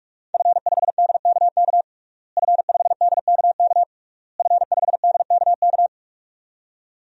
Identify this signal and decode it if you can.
Morse code, heard as a keyed tone, about 46 wpm, 700 Hz. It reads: F5DKK F5DKK F5DKK
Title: DXCC-100 Call Signs - 5 Characters - Repeated 3x - 45wpm